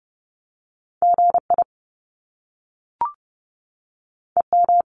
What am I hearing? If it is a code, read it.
Morse code, heard as a keyed tone, about 30 wpm, 700 Hz. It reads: GI EM
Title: Top Two Letter Combinations in Words 30wpm